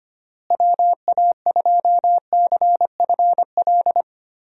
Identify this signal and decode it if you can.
Morse code, heard as a keyed tone, about 25 wpm, 700 Hz. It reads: WA2CFL
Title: USA Call Signs - 6 Characters - Repeated 3x - 25wpm